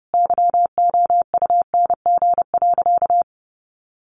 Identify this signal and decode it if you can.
Morse code, heard as a keyed tone, about 30 wpm, 700 Hz. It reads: YOUNG.
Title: Sentences from Top 400 Words 30wpm